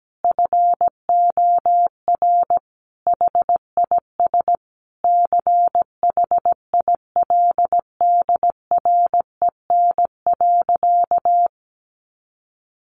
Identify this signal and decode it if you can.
Morse code, heard as a keyed tone, about 17 wpm, 700 Hz. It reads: FOR HIS CHILDREN.
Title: Sentences from Top 1100 Words 17wpm